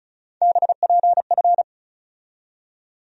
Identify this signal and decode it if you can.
Morse code, heard as a keyed tone, about 34 wpm, 700 Hz. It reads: BPF
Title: USA Call Sign Postfixes - 3 Letters 35wpm